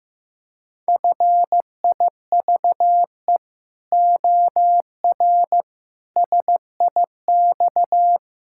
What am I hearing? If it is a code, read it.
Morse code, heard as a keyed tone, about 16 wpm, 700 Hz. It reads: FIVE OR SIX
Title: Pangram Challenge v2 - 15wpm